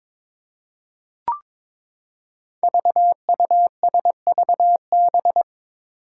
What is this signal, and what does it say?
Morse code, heard as a keyed tone, about 22 wpm, 700 Hz. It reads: VUSVB
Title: Five Challenging Letter-Number Combinations 22wpm